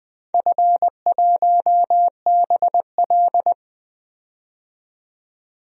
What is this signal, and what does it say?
Morse code, heard as a keyed tone, about 20 wpm, 700 Hz. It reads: F1BL
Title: DXCC-100 Call Signs - 4 Characters - 20wpm